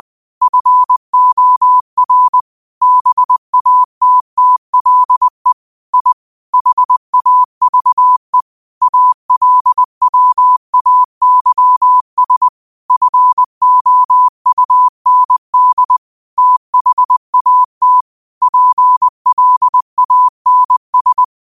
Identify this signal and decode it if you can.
Morse code, heard as a keyed tone, about 20 wpm, 1000 Hz. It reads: FOR BATTLE I HAVE ALWAYS FOUND THAT PLANS